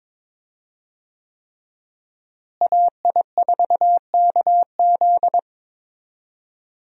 Morse code, heard as a keyed tone, about 22 wpm, 690 Hz. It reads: AI4KZ
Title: USA Call Signs - 5 Characters - Repeated 3x - 22wpm